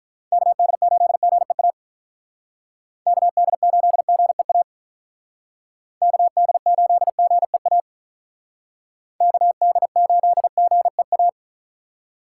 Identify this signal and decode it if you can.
Morse code, heard as a keyed tone, about 45 wpm, 700 Hz. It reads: KD8GEA KD8GEA KD8GEA KD8GEA